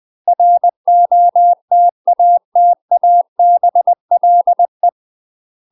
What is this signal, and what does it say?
Morse code, heard as a keyed tone, about 20 wpm, 700 Hz. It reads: ROTATABLE
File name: UK MASTER QSOs TRAINING / 20WPM / UK MASTER QSOs 0002